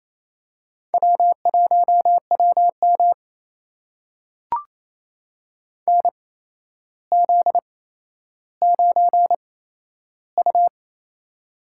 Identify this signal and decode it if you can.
Morse code, heard as a keyed tone, about 28 wpm, spaced as for 12 wpm, 700 Hz. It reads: W1WM NZ9U
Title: USA Call Signs - 4 Characters - 8x Character Spacing - 28wpm